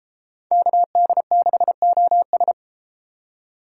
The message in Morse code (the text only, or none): KD6OS